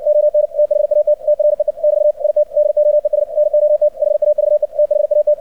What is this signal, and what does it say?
Morse code, heard as a keyed tone, about 60 wpm, 600 Hz. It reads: ON THAT TREE <BT> IT DOES NOT HAVE THAT
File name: archived / CRSnet